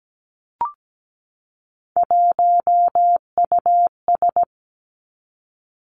Morse code, heard as a keyed tone, about 17 wpm, 700 Hz. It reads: 1US
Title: Three Challenging Letter-Number Combinations 17wpm